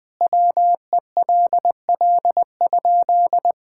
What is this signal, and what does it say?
Morse code, heard as a keyed tone, about 20 wpm, 700 Hz. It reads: WELL?
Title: Sentences from Top 1000 Words - Review Part 6 - 20wpm